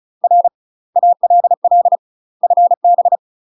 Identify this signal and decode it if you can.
Morse code, heard as a keyed tone, about 36 wpm, 700 Hz. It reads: R ALL FB
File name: UK MASTER QSOs TRAINING / 35WPM / UK MASTER QSOs 0013